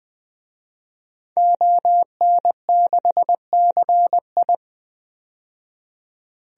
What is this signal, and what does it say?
Morse code, heard as a keyed tone, about 20 wpm, 700 Hz. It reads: ON6CI
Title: DXCC-100 Call Signs - 5 Characters - 20wpm